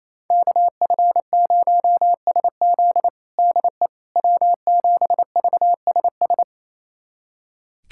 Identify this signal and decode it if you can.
Morse code, heard as a keyed tone, about 27 wpm, 700 Hz. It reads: KF0SZ DE W7VSS
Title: USA Callsign DE Callsign - 5 Characters - 28wpm